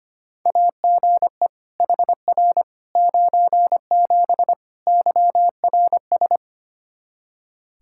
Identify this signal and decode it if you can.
Morse code, heard as a keyed tone, about 25 wpm, 700 Hz. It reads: AGE HR 97 YRS
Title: Practice QSO - Exchange 3 - More Info - 25wpm